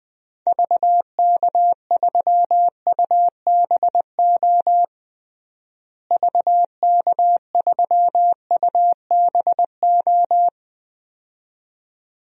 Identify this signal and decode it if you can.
Morse code, heard as a keyed tone, about 21 wpm, 700 Hz. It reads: VK3UBO VK3UBO
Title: DXCC-100 Call Signs - 6 Characters - Repeated 2x - 20wpm